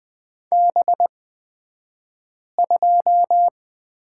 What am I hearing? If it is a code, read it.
Morse code, heard as a keyed tone, about 20 wpm, spaced as for 5 wpm, 700 Hz. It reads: B2
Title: 223 Lesson 36d - BT - Two Characters 20wpm